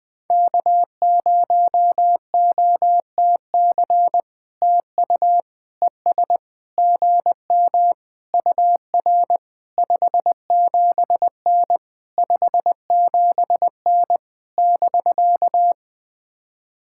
Morse code, heard as a keyed tone, about 20 wpm, 700 Hz. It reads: K0OTC TU ES GM UR 57N 57N <BK>
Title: POTA QSO Practice - 20wpm